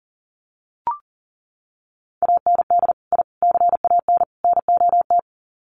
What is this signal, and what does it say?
Morse code, heard as a keyed tone, about 39 wpm, 700 Hz. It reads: AND I CAN NOT
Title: Sets of 4 Words 40wpm